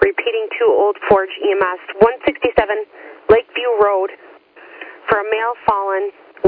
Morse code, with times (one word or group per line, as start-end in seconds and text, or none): none